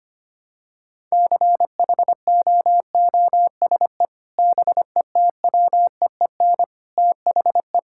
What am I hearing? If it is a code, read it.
Morse code, heard as a keyed tone, about 25 wpm, 700 Hz. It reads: CHOOSE BETWEEN THE
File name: practice / 25wpm / Sentences with 3-Word Phrases - 25wpm